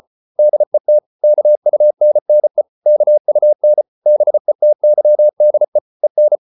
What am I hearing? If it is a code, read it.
Morse code, heard as a keyed tone, about 34 wpm, 600 Hz. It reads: DET KUNNE KUN BETYDE EN
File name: Ferie QRQ 34wpm